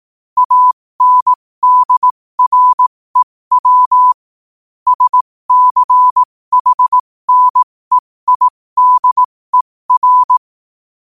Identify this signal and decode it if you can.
Morse code, heard as a keyed tone, about 18 wpm, spaced as for 15 wpm, 1000 Hz. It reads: ANDREW SCHNEIDER
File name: qotdfiles / 15wpm